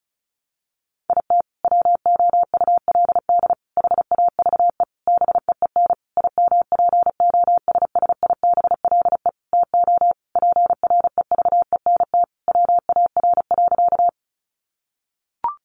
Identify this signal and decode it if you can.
Morse code, heard as a keyed tone, about 34 wpm, 700 Hz. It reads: IT WOULD HAVE BEEN IMPOSSIBLE TO PREVENT WAR.
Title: Sentences from Top 1100 Words 35wpm